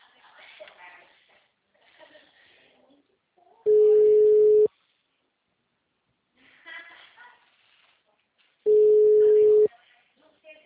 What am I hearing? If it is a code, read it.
Ringback tone (Europe, 425 Hz), with speech.